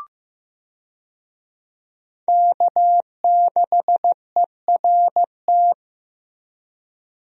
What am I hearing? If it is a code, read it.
Morse code, heard as a keyed tone, about 15 wpm, 700 Hz. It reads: K6ERT
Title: USA Call Signs - 5 Characters - Repeated 3x - 15wpm